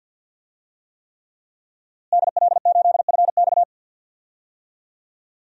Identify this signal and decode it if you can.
Morse code, heard as a keyed tone, about 53 wpm, 700 Hz. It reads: DL8FX
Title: DXCC-100 Call Signs - 5 Characters - Repeated 3x - 50wpm